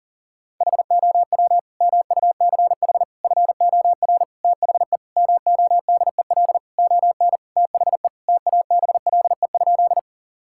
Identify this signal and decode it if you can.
Morse code, heard as a keyed tone, about 40 wpm, 700 Hz. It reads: HOW MUCH FOR THE MODEL ON THE TABLE?
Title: Sentences from Top 1000 Words - Review Part 5 - 40wpm